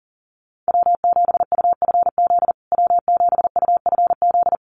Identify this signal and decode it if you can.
Morse code, heard as a keyed tone, about 40 wpm, 700 Hz. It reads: W7UFZ W7UFZ
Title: USA Call Signs Repeated 3x 40wpm